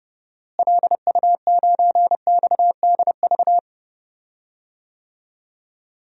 Morse code, heard as a keyed tone, about 30 wpm, 700 Hz. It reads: LU9XDV
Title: DXCC-100 Call Signs - 6 Characters - Repeated 3x - 30wpm